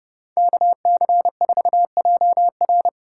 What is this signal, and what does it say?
Morse code, heard as a keyed tone, about 38 wpm, 700 Hz. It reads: KC4JR